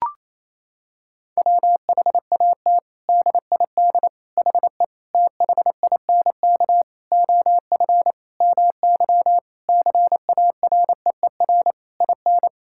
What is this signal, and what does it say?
Morse code, heard as a keyed tone, about 28 wpm, 700 Hz. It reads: WHAT DID HE THINK OF MY CAREER IN
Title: Sentences from Top 1000 Words - Review Part 6 - 28wpm